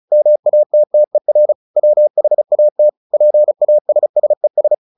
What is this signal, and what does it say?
Morse code, heard as a keyed tone, about 34 wpm, 600 Hz. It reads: MATTER WHAT PASSES
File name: the old man and the sea 2 35wpm